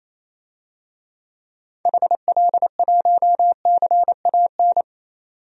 Morse code, heard as a keyed tone, about 28 wpm, 700 Hz. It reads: HL1CAN
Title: DXCC-100 Call Signs - 6 Characters - 28wpm